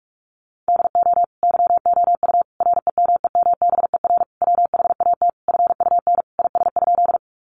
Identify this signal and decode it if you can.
Morse code, heard as a keyed tone, about 44 wpm, 700 Hz. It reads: DO YOU REMEMBER WHAT FUN IS?
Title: Sentences from Top 700 Words 45wpm